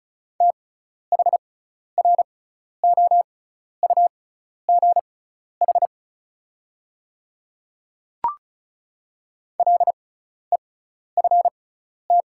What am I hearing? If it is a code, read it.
Morse code, heard as a keyed tone, about 36 wpm, spaced as for 12 wpm, 690 Hz. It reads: THROUGH LEFT
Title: 6x Character Spacing - Top 300 Words - 35wpm